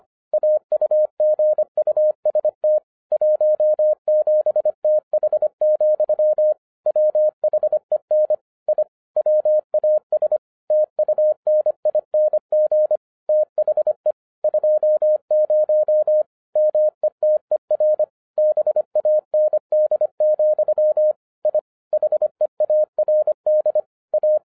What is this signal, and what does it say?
Morse code, heard as a keyed tone, about 25 wpm, 610 Hz. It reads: AUGUST 17TH, WHEN I WAS TUNING THE 20 METER BAND, I HEARD A
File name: 2012-01-mp3 / n06-FIRST FISTS CONTACT-25wpm